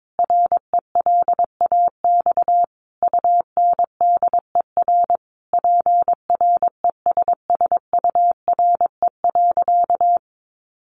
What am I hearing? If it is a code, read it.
Morse code, heard as a keyed tone, about 22 wpm, 700 Hz. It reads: RELAX UNDER PRESSURE.
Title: Sentences from Top 1400 Words 22wpm